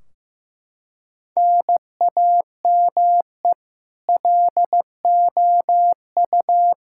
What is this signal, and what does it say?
Morse code, heard as a keyed tone, about 15 wpm, 700 Hz. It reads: NAME LOU
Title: Practice QSO - Exchange 1 - Beginning - 15wpm